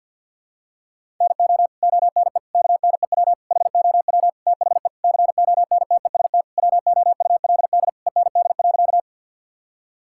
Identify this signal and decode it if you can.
Morse code, heard as a keyed tone, about 50 wpm, 700 Hz. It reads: NO ONE KNEW HOW THE CONTEST WOULD END.
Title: Sentences from Top 1600 Words 50wpm